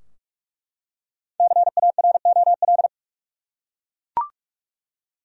Morse code, heard as a keyed tone, about 43 wpm, 700 Hz. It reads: CAROL
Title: USA Common Names - 45wpm